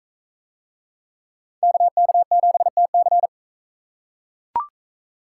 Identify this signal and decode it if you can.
Morse code, heard as a keyed tone, about 43 wpm, 700 Hz. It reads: KK7TC